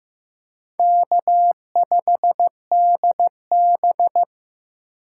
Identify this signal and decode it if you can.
Morse code, heard as a keyed tone, about 15 wpm, 700 Hz. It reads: K5DB
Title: USA Call Signs - 4 Characters - 6x Character Spacing - 15wpm